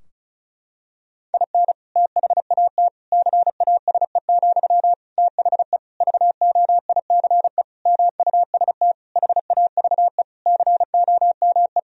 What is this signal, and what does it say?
Morse code, heard as a keyed tone, about 36 wpm, 700 Hz. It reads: IN THAT CASE, THE VOICE MUST HAVE COME
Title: Sentences from Top 1000 Words - Review Part 2 - 35wpm